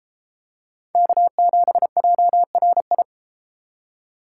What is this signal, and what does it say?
Morse code, heard as a keyed tone, about 30 wpm, 700 Hz. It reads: K7JRI